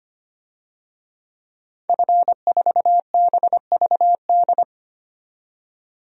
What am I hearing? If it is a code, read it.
Morse code, heard as a keyed tone, about 25 wpm, 700 Hz. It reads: F4BVD
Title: DXCC-100 Call Signs - 5 Characters - 25wpm